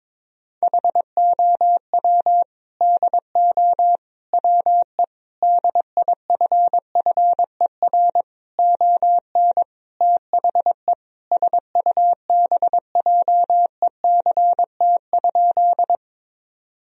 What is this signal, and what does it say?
Morse code, heard as a keyed tone, about 21 wpm, 700 Hz. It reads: HOW DO WE DIFFER ON THE SUBJECT?
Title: Sentences from Top 1000 Words - Review Part 1 - 22wpm